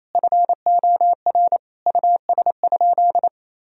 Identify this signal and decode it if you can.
Morse code, heard as a keyed tone, about 28 wpm, 700 Hz. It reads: FOR US?
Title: Sentences from Top 1000 Words - Review Part 3 - 28wpm